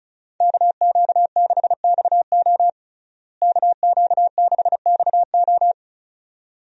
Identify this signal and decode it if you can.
Morse code, heard as a keyed tone, about 34 wpm, 700 Hz. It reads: KQ6XO KQ6XO
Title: USA Call Signs - 5 Characters - Repeated 2x - 35wpm